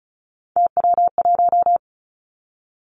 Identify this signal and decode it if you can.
Morse code, heard as a keyed tone, about 36 wpm, 700 Hz. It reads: TW1
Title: Three Challenging Letter-Number Combinations 35wpm